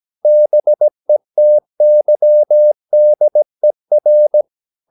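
Morse code, heard as a keyed tone, about 17 wpm, 600 Hz. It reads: BETYDER
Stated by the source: Slow (QRS) Morse practice at 17 wpm, from CW med Gnister.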